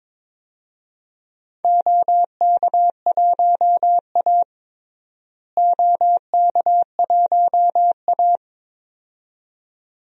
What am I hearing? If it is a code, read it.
Morse code, heard as a keyed tone, about 22 wpm, 700 Hz. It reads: OK1A OK1A
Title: DXCC-100 Call Signs - 4 Characters - Repeated 2x - 22wpm